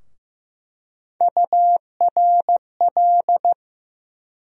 Morse code, heard as a keyed tone, about 15 wpm, 700 Hz. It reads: URL
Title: USA Call Sign Postfixes - 3 Letters 15wpm